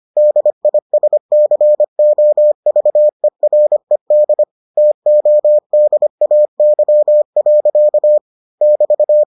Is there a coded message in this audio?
Yes: Morse code from 0.2 to 9.2 s, heard as a keyed tone, about 25 wpm, 600 Hz. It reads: DISCOVERED TODAY. <BT>